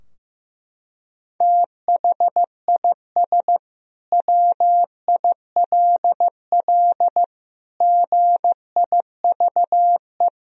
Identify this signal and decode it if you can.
Morse code, heard as a keyed tone, about 15 wpm, 700 Hz. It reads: THIS WILL GIVE
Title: Sentences from Top 1000 Words - Review Part 3 - 15wpm